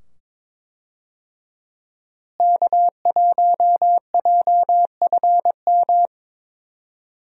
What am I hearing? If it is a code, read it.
Morse code, heard as a keyed tone, about 22 wpm, 700 Hz. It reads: K1JFM
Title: USA Call Signs - 5 Characters - Repeated 2x - 22wpm